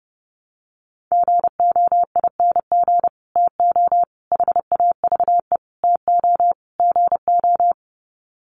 Morse code, heard as a keyed tone, about 30 wpm, 700 Hz. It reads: GOING TO HAVE TO GO
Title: Sets of 5 Words Encore 30wpm